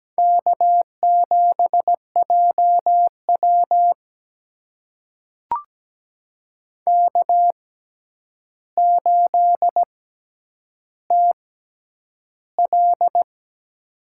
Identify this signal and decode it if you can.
Morse code, heard as a keyed tone, about 17 wpm, spaced as for 9 wpm, 700 Hz. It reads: K7JW K8TL
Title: USA Call Signs - 4 Characters - 6x Character Spacing - 17wpm